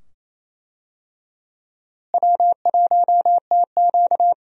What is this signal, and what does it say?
Morse code, heard as a keyed tone, about 28 wpm, 700 Hz. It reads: W1TQ
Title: USA Call Signs - 4 Characters - Repeated 2x - 28wpm